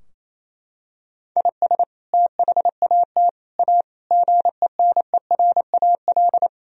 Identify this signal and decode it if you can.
Morse code, heard as a keyed tone, about 27 wpm, 700 Hz. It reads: IS THAT A GENERAL
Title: Sentences from Top 1000 Words - Review Part 3 - 28wpm